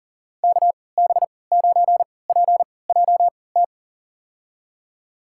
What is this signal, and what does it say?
Morse code, heard as a keyed tone, about 40 wpm, spaced as for 23 wpm, 700 Hz. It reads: KB9PJT
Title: USA Call Signs - 3x Character Spacing - 40wpm